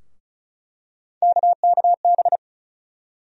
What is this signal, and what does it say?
Morse code, heard as a keyed tone, about 36 wpm, 700 Hz. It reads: KKB
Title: USA Call Sign Postfixes - 3 Letters 35wpm